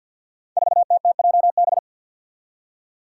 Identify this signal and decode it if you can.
Morse code, heard as a keyed tone, about 47 wpm, 700 Hz. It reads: 4TTJB
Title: Five Challenging Letter-Number Combinations 50wpm